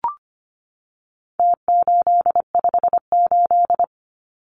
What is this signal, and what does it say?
Morse code, heard as a keyed tone, about 25 wpm, 700 Hz. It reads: T858
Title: Numbers - Contest Style - 4-Digit 25wpm